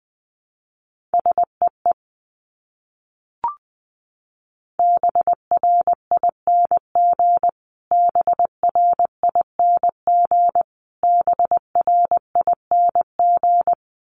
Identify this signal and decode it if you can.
Morse code, heard as a keyed tone, about 20 wpm, 700 Hz. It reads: SEE BRING BRING BRING
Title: Top 500 Words Repeated 20wpm